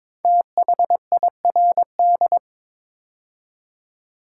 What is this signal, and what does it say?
Morse code, heard as a keyed tone, about 23 wpm, 700 Hz. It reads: THIRD
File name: 5 Letter Words - First 3 Letters Given - 22wpm